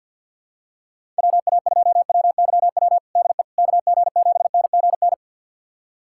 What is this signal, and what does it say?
Morse code, heard as a keyed tone, about 49 wpm, 700 Hz. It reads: WA2WYW DE KG7NGN
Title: USA Callsign DE Callsign - 6 Characters - 50wpm